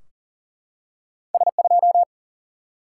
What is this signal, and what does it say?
Morse code, heard as a keyed tone, about 40 wpm, 700 Hz. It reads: S2
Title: Single-Letter Single-Number 40wpm